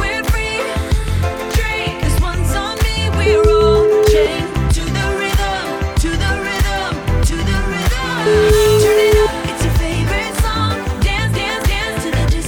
Ringback tone (Europe, 425 Hz), with noise.